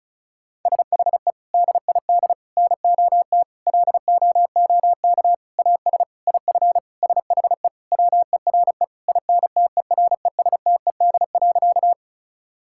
Morse code, heard as a keyed tone, about 34 wpm, 700 Hz. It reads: SHE DID NOT LOOK AS IF SHE WERE INTERESTED.
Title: Sentences from Top 1000 Words - Review Part 6 - 35wpm